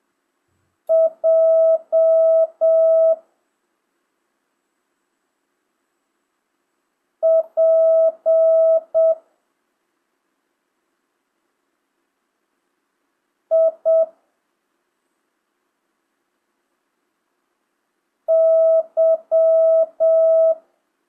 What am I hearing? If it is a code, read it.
Morse code, heard as a keyed tone, about 7 wpm, 650 Hz. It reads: J P I Y
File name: jh8cbh / 2025cw